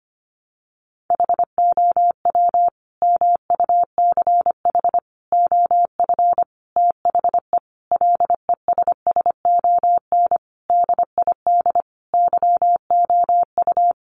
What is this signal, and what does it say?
Morse code, heard as a keyed tone, about 25 wpm, 700 Hz. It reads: HOW MUCH OF THE LESSON DID YOU
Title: Sentences from Top 1200 Words 25wpm